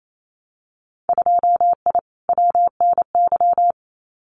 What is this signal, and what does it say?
Morse code, heard as a keyed tone, about 28 wpm, 700 Hz. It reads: 2I WNY
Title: Winter Field Day Exchanges 28wpm